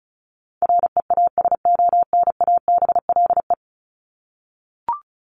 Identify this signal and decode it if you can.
Morse code, heard as a keyed tone, about 34 wpm, 700 Hz. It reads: REASONABLE
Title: Top 1501-1750 Words 35wpm